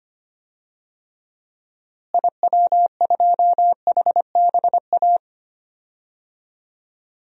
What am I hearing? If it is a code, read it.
Morse code, heard as a keyed tone, about 25 wpm, 700 Hz. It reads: IW2HBA
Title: DXCC-100 Call Signs - 6 Characters - Repeated 3x - 25wpm